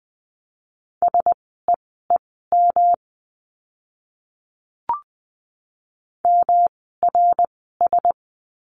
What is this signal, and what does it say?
Morse code, heard as a keyed tone, about 20 wpm, spaced as for 14 wpm, 700 Hz. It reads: SEEM MRS
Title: 108 Lesson 18f - Letter m - Words 20wpm